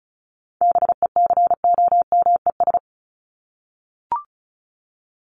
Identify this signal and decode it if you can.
Morse code, heard as a keyed tone, about 34 wpm, 700 Hz. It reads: BECOMES
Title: ES:S Suffix - Top 100 Words with Base Words 35wpm